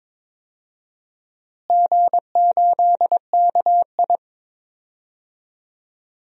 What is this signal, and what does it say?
Morse code, heard as a keyed tone, about 22 wpm, 700 Hz. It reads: G8KI
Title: DXCC-100 Call Signs - 4 Characters - Repeated 2x - 22wpm